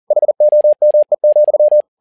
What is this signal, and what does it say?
Morse code, heard as a keyed tone, about 40 wpm, 600 Hz. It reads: HOME,
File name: learning center / tower trouble / tower touble 40wpm